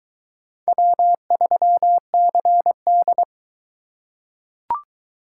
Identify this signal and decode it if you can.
Morse code, heard as a keyed tone, about 22 wpm, 700 Hz. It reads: W3CD